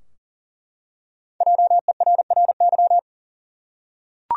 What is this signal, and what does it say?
Morse code, heard as a keyed tone, about 38 wpm, 700 Hz. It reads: JERRY
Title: USA Common Names - Max 5 Letters - 40wpm